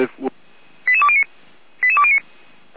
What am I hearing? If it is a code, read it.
Five-tone selective call (ZVEI), heard as quick runs of five tones, tones of about 70 ms, with speech. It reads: 88209 88209